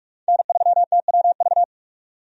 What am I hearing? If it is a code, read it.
Morse code, heard as a keyed tone, about 46 wpm, 700 Hz. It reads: N3TWV
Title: USA Call Signs - 5 Characters - 45wpm